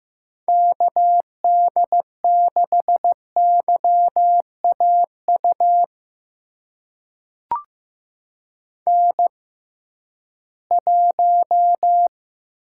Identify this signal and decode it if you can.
Morse code, heard as a keyed tone, about 15 wpm, spaced as for 11 wpm, 700 Hz. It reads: KD6YAU N1
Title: USA Call Signs - 6x Character Spacing - 15wpm